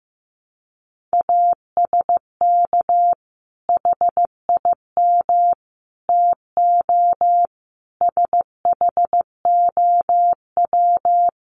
Morse code, heard as a keyed tone, about 15 wpm, 700 Hz. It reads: ASK HIM TO SHOW
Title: Sentences from Top 200 Words 15wpm